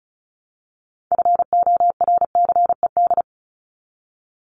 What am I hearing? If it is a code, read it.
Morse code, heard as a keyed tone, about 34 wpm, 700 Hz. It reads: FORCED
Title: ED Suffix - Top 100 Words with Base Words 35wpm